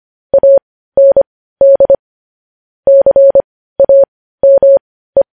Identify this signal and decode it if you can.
Morse code, heard as a keyed tone, about 25 wpm, spaced as for 15 wpm, 550 Hz. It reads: AND CAME
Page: news headlines in morse code at 15 wpm